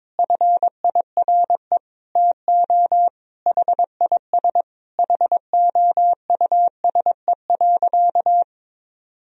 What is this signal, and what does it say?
Morse code, heard as a keyed tone, about 22 wpm, 700 Hz. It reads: FIRE TO HIS HOUSE.
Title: Sentences from Top 1000 Words - Review Part 1 - 22wpm